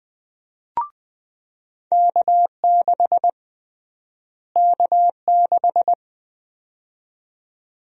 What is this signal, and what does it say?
Morse code, heard as a keyed tone, about 21 wpm, 700 Hz. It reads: K6 K6
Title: USA Call Sign Prefixes - Repeated 2x - 20wpm